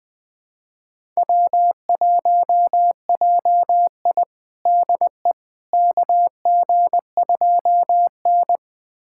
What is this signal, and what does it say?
Morse code, heard as a keyed tone, about 21 wpm, 700 Hz. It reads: W1JI DE KG2N
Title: USA Callsign DE Callsign - 4 Characters - 20wpm